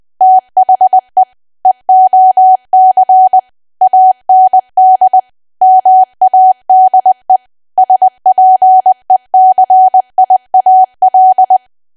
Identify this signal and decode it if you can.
Morse code, heard as a keyed tone, about 20 wpm, 750 Hz. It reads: THE EOC AND MADE SPECIAL
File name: Archive / 20 WPM / 20WPM